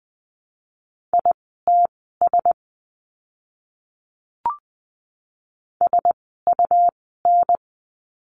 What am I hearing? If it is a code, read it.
Morse code, heard as a keyed tone, about 20 wpm, spaced as for 15 wpm, 700 Hz. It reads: ITS SUN
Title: 101 Lesson 17f - Letter u - Words 20wpm